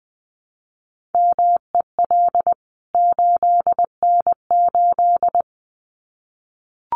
Morse code, heard as a keyed tone, about 20 wpm, 700 Hz. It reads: MEL 8N8
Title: CWT Exchange 20wpm